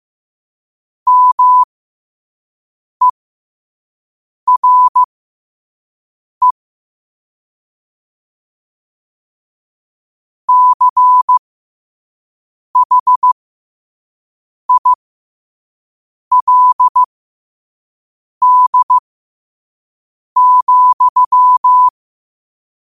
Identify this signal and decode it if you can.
Morse code, heard as a keyed tone, about 15 wpm, spaced as for 5 wpm, 1000 Hz. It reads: MERE CHILD,
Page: New quotes every day in morse code at 5 Words per minute.